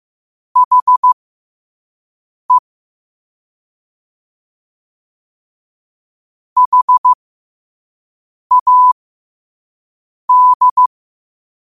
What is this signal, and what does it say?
Morse code, heard as a keyed tone, about 15 wpm, spaced as for 5 wpm, 1000 Hz. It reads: HE HAD
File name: qotdfiles / 5wpm